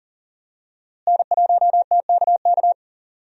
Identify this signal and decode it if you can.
Morse code, heard as a keyed tone, about 38 wpm, 700 Hz. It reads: N1TKK
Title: USA Call Signs - 5 Characters - Repeated 3x - 40wpm